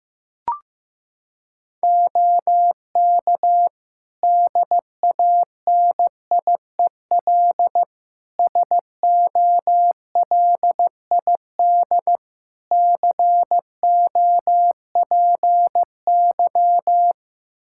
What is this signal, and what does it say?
Morse code, heard as a keyed tone, about 16 wpm, 700 Hz. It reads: OK DANIEL SOLID COPY
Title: K1USN SST QSO Practice 15wpm